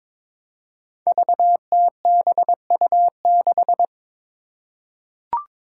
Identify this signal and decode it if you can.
Morse code, heard as a keyed tone, about 21 wpm, 700 Hz. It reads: VTBU6
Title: Five Challenging Letter-Number Combinations 22wpm